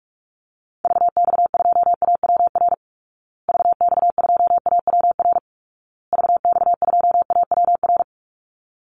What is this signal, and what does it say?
Morse code, heard as a keyed tone, about 44 wpm, 700 Hz. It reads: VX2AWR VX2AWR VX2AWR
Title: Call Signs Repeated 2x - 45wpm